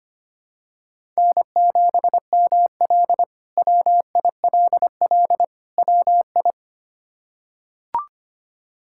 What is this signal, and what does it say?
Morse code, heard as a keyed tone, about 25 wpm, 700 Hz. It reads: N7ML WILL WI
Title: K1USN SST QSO Practice - 25wpm